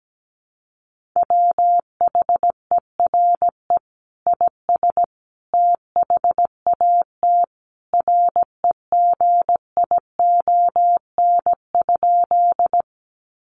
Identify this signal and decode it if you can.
Morse code, heard as a keyed tone, about 17 wpm, 700 Hz. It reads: WHERE IS THAT REGION?
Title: Sentences from Top 1500 Words 17wpm